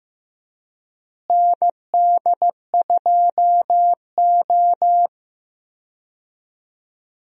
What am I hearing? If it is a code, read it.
Morse code, heard as a keyed tone, about 15 wpm, 700 Hz. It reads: ND2O
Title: USA Call Signs - 4 Characters - Repeated 2x - 15wpm